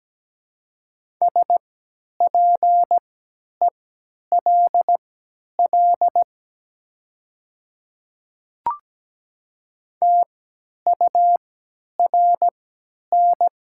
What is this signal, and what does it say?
Morse code, heard as a keyed tone, about 17 wpm, spaced as for 10 wpm, 700 Hz. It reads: SPELL TURN
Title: 3x Character Spacing - Top 300 Words - 17wpm